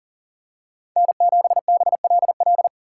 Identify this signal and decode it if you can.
Morse code, heard as a keyed tone, about 40 wpm, 700 Hz. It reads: N7BLL
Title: USA Call Signs - 5 Characters - 4x Character Spacing - 40wpm